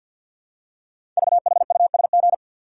Morse code, heard as a keyed tone, about 48 wpm, 700 Hz. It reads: VHUSG
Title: Code Groups - 5 Letters - 50wpm